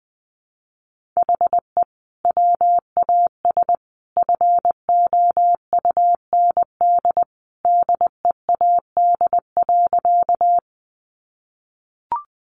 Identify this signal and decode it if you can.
Morse code, heard as a keyed tone, about 20 wpm, 700 Hz. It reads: HE WAS FOUND DEAD.
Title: Sentences from Top 1500 Words 20wpm